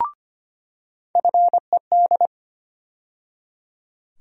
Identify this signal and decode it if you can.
Morse code, heard as a keyed tone, about 25 wpm, 700 Hz. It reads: FED
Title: USA Call Sign Postfixes - 3 Letters 25wpm